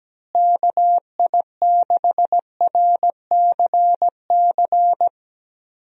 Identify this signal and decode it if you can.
Morse code, heard as a keyed tone, about 17 wpm, 700 Hz. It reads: KI6RCC
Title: USA Call Signs - 6 Characters - Repeated 2x - 17wpm